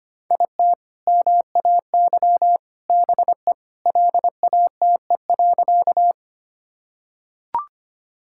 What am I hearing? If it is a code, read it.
Morse code, heard as a keyed tone, about 25 wpm, 700 Hz. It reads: IT MAY BE LATE.
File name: practice / 25wpm / Sentences with 3-Word Phrases - 25wpm